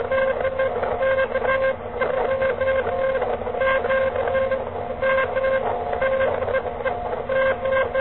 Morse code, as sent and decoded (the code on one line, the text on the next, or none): none